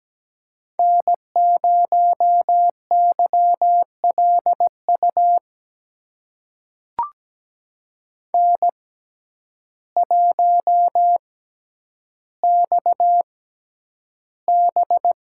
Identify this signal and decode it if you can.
Morse code, heard as a keyed tone, about 17 wpm, spaced as for 9 wpm, 700 Hz. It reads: N0YLU N1XB
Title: USA Call Signs - 5 Characters - 6x Character Spacing - 17wpm